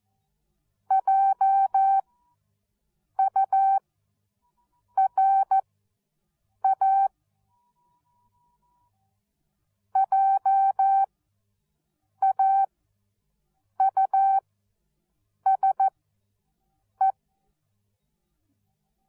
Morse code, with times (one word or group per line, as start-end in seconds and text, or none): 0.9-7.1 JURA
10.0-17.1 JAUSE